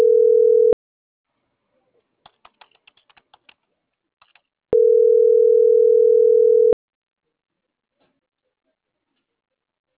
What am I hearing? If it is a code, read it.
Ringback tone (North America), 440 + 480 Hz.